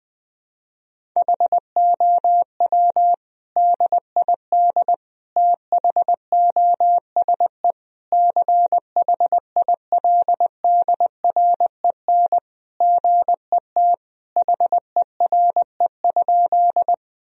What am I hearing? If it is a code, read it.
Morse code, heard as a keyed tone, about 20 wpm, 700 Hz. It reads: HOW DID THOSE CHILDREN GET HERE?
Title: Sentences from Top 1000 Words - Review Part 1 - 20wpm